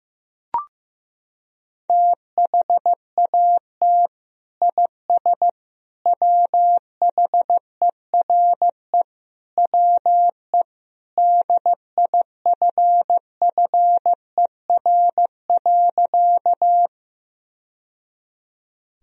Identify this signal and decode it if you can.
Morse code, heard as a keyed tone, about 16 wpm, 700 Hz. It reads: THAT IS WHERE WE DIFFER.
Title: Sentences from Top 1000 Words - Review Part 1 - 15wpm